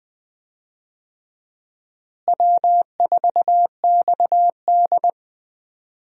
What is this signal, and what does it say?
Morse code, heard as a keyed tone, about 20 wpm, 700 Hz. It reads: W4XD
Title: USA Call Signs - 4 Characters - Repeated 3x - 20wpm